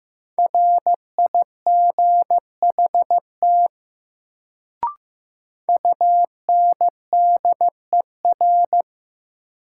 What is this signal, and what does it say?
Morse code, heard as a keyed tone, about 15 wpm, 700 Hz. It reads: RIGHT UNDER
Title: Sets of 2 Words - New Word Drill - 15wpm